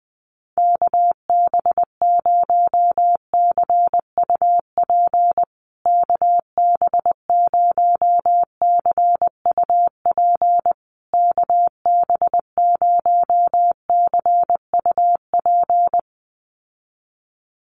Morse code, heard as a keyed tone, about 20 wpm, 700 Hz. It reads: KB0CUP KB0CUP KB0CUP
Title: USA Call Signs Repeated 3x 20wpm